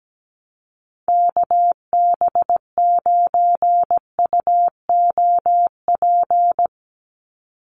Morse code, heard as a keyed tone, about 17 wpm, 700 Hz. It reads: KB9UOP
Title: USA Call Signs 17wpm